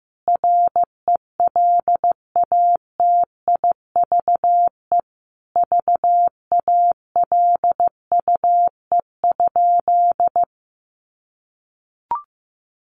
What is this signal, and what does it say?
Morse code, heard as a keyed tone, about 15 wpm, 700 Hz. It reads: RELATIVE VALUE?
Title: Sentences from Top 1400 Words 15wpm